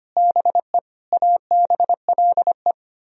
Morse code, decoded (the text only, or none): BE ABLE